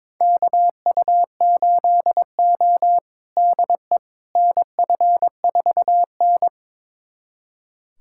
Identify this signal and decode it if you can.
Morse code, heard as a keyed tone, about 22 wpm, 700 Hz. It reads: KU8O DE NF4N
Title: USA Callsign DE Callsign - 4 Characters - 22wpm